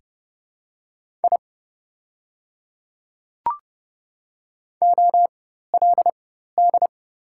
Morse code, heard as a keyed tone, about 30 wpm, spaced as for 14 wpm, 700 Hz. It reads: I OLD
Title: 4x Character Spacing - Top 300 Words - 30wpm